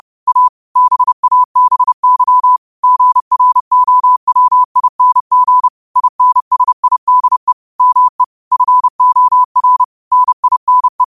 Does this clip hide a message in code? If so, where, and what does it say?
Yes: Morse code from 0.3 to 11.0 s, heard as a keyed tone, about 30 wpm, 1000 Hz. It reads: A BABY GROWING INSIDE ME FOR NINE